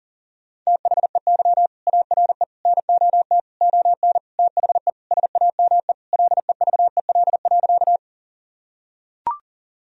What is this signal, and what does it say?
Morse code, heard as a keyed tone, about 40 wpm, 700 Hz. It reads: THEY ARE NOT ON THE SAME LEVEL.
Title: Sentences from Top 1000 Words - Review Part 4 - 40wpm